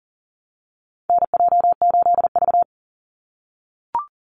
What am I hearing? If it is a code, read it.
Morse code, heard as a keyed tone, about 41 wpm, 700 Hz. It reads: NJ8V
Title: Call Signs - 40wpm